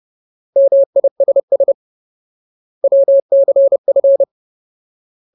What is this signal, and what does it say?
Morse code, heard as a keyed tone, about 30 wpm, 550 Hz. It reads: MISS WCF
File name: Ultimate Word List / 30WPM Ultimate